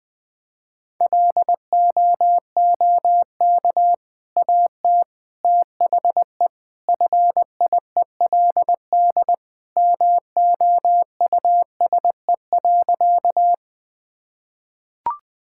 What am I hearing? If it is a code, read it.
Morse code, heard as a keyed tone, about 20 wpm, 700 Hz. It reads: LOOK AT THE FIELD MOUSE.
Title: Sentences from Top 1000 Words - Review Part 5 - 20wpm